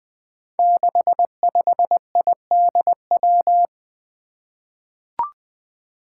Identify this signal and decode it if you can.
Morse code, heard as a keyed tone, about 20 wpm, 700 Hz. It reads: 65IDW
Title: Five Challenging Letter-Number Combinations 20wpm